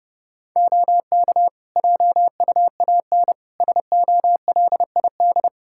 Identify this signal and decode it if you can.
Morse code, heard as a keyed tone, about 30 wpm, 700 Hz. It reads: OK JUAN SOLID
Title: Practice QSO - Exchange 3 - More Info - 30wpm